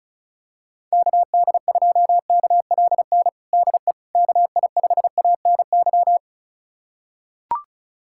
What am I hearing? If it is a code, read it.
Morse code, heard as a keyed tone, about 36 wpm, 700 Hz. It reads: KD2KLN DE KI5ANY
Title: USA Callsign DE Callsign - 6 Characters - 35wpm